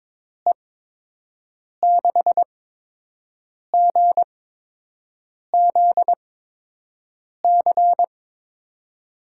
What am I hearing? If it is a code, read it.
Morse code, heard as a keyed tone, about 22 wpm, spaced as for 6 wpm, 700 Hz. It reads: E6GZC
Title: USA Call Signs - 8x Character Spacing - 22wpm